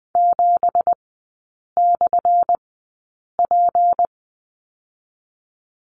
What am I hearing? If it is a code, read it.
Morse code, heard as a keyed tone, about 20 wpm, 700 Hz. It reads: 7 / P
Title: 178 Lesson 29e - Slash - Three Characters 20wpm